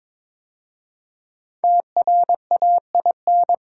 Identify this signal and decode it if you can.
Morse code, heard as a keyed tone, about 22 wpm, 700 Hz. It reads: TRAIN
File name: 5 Letter Words - First 3 Letters Given - 22wpm